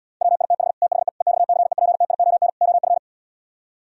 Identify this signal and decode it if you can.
Morse code, heard as a keyed tone, about 77 wpm, 700 Hz. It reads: WITH THE EXCEPTION OF